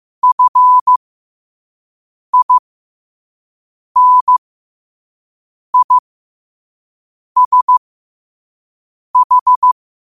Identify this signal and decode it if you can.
Morse code, heard as a keyed tone, about 15 wpm, spaced as for 5 wpm, 1000 Hz. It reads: FINISH
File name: 5wpm